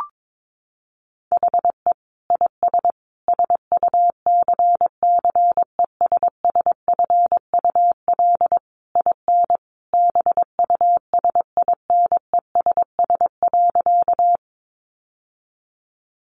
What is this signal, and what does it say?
Morse code, heard as a keyed tone, about 22 wpm, 700 Hz. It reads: HE IS SUCCESSFUL IN BUSINESS.
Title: Sentences from Top 900 Words 22wpm